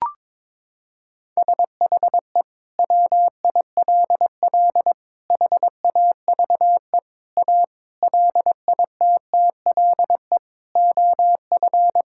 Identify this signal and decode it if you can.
Morse code, heard as a keyed tone, about 22 wpm, 700 Hz. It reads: SHE WILL HAVE A LITTLE OF
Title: Sentences from Top 1000 Words - Review Part 5 - 22wpm